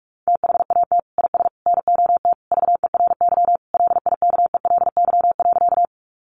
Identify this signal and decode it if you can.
Morse code, heard as a keyed tone, about 46 wpm, 700 Hz. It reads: THAT IS NOT VERY LIKELY.
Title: Sentences from Top 600 Words 45wpm